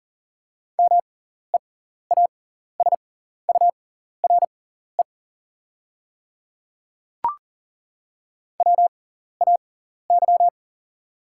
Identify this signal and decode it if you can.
Morse code, heard as a keyed tone, about 38 wpm, spaced as for 14 wpm, 690 Hz. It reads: MEASURE WAY
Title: 6x Character Spacing - Top 500 Words - 40wpm